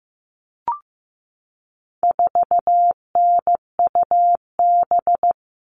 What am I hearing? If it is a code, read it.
Morse code, heard as a keyed tone, about 15 wpm, 700 Hz. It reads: 4NUB
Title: All Cut Numbers - 4-Digit 15wpm